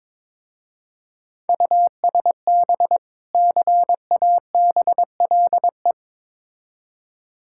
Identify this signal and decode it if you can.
Morse code, heard as a keyed tone, about 22 wpm, 690 Hz. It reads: USB CABLE
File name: Difficult Words with Hints - 22wpm